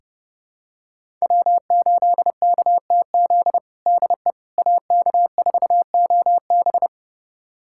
Morse code, heard as a keyed tone, about 30 wpm, 700 Hz. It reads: W8KTZ DE AK4OB
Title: USA Callsign DE Callsign - 5 Characters - 30wpm